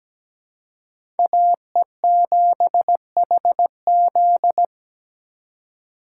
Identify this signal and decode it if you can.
Morse code, heard as a keyed tone, about 17 wpm, 700 Hz. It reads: AE7HZ
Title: USA Call Signs - 5 Characters - 2x Character Spacing - 17wpm